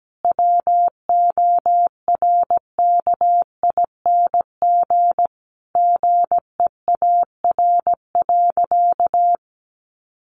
Morse code, heard as a keyed tone, about 17 wpm, 700 Hz. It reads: WORKING GEAR.
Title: Sentences from Top 1400 Words 17wpm